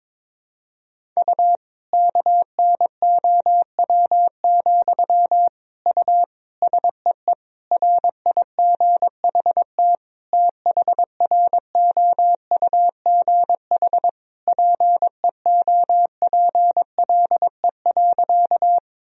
Morse code, heard as a keyed tone, about 21 wpm, 700 Hz. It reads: U KNOW, U SEE RIGHT THROUGH PEOPLE.
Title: Sentences from Top 200 Words and Abbreviations 22wpm